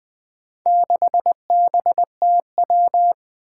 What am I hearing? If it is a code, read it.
Morse code, heard as a keyed tone, about 20 wpm, 700 Hz. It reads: 6BTW
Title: Four Challenging Letter-Number Combinations 20wpm